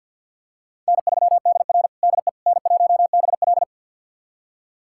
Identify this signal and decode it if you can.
Morse code, heard as a keyed tone, about 52 wpm, 700 Hz. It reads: N3DR DE N1BL
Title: USA Callsign DE Callsign - 4 Characters - 50wpm